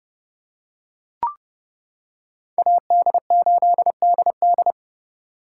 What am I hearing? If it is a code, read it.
Morse code, heard as a keyed tone, about 28 wpm, 700 Hz. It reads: AD8DD